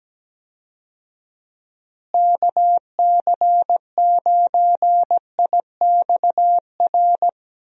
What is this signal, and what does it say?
Morse code, heard as a keyed tone, about 17 wpm, 700 Hz. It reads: KC9IXR
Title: USA Call Signs - 6 Characters - Repeated 3x - 17wpm